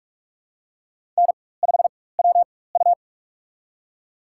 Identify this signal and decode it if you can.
Morse code, heard as a keyed tone, about 46 wpm, spaced as for 21 wpm, 700 Hz. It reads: N5WU
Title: USA Call Signs - 4 Characters - 4x Character Spacing - 45wpm